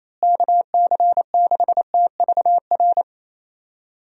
Morse code, heard as a keyed tone, about 28 wpm, 700 Hz. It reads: KC6TVR
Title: USA Call Signs - 6 Characters - Repeated 3x - 28wpm